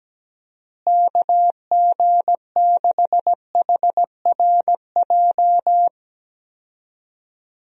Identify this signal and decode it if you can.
Morse code, heard as a keyed tone, about 17 wpm, 700 Hz. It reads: KG6HRJ
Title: USA Call Signs - 6 Characters - Repeated 3x - 17wpm